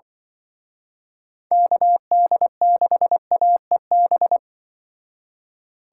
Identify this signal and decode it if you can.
Morse code, heard as a keyed tone, about 24 wpm, 700 Hz. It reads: KD6AEB
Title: Speed Racing - USA 2x3 Call Signs 20wpm